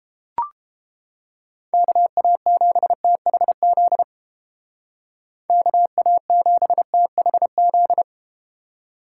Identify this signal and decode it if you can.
Morse code, heard as a keyed tone, about 32 wpm, 700 Hz. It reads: KA7THZ KA7THZ